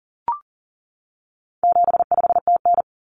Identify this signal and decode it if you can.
Morse code, heard as a keyed tone, about 40 wpm, 700 Hz. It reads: 75TN
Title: Common Cut Numbers - 4-Digit 40wpm